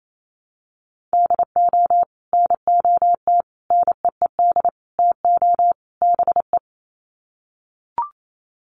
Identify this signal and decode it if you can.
Morse code, heard as a keyed tone, about 28 wpm, 700 Hz. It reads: DO NOT NEED TO BE
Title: Sets of 5 Words Encore 28wpm